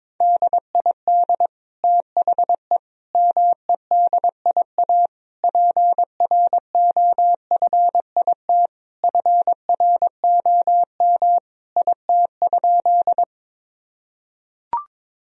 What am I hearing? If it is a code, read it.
Morse code, heard as a keyed tone, about 22 wpm, 700 Hz. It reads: DID THE MEDIA PROFIT FROM IT?
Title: Sentences from Top 1000 Words - Review Part 4 - 22wpm